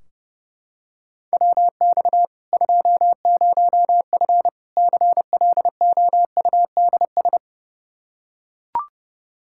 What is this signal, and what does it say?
Morse code, heard as a keyed tone, about 30 wpm, 700 Hz. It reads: WX 20F CLOUDS
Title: Practice QSO - Exchange 2 - Info - 30wpm